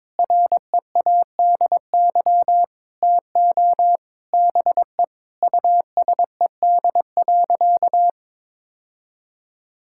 Morse code, heard as a keyed tone, about 22 wpm, 700 Hz. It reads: READY TO BE USED.
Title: Sentences from Top 1600 Words 22wpm